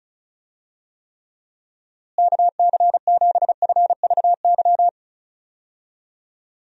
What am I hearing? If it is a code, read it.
Morse code, heard as a keyed tone, about 36 wpm, 700 Hz. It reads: KC7FVY
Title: USA Call Signs - 6 Characters - Repeated 2x - 35wpm